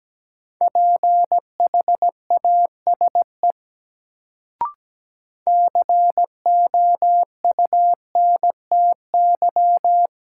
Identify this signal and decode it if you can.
Morse code, heard as a keyed tone, about 17 wpm, 700 Hz. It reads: PHASE COUNTY
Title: Sentences from Top 1200 Words - New Word Drill - 17wpm